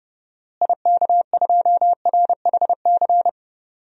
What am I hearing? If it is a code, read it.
Morse code, heard as a keyed tone, about 30 wpm, 700 Hz. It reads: IK2RHC
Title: DXCC-100 Call Signs - 6 Characters - Repeated 2x - 30wpm